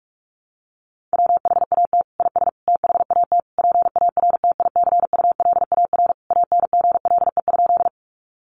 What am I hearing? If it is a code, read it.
Morse code, heard as a keyed tone, about 44 wpm, 700 Hz. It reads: WHAT IS THAT PARTICULAR ANGLE?
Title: Sentences from Top 1500 Words 45wpm